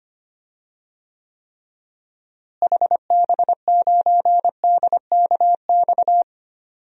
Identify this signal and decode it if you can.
Morse code, heard as a keyed tone, about 25 wpm, 700 Hz. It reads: HB9DKX
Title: DXCC-100 Call Signs - 6 Characters - Repeated 3x - 25wpm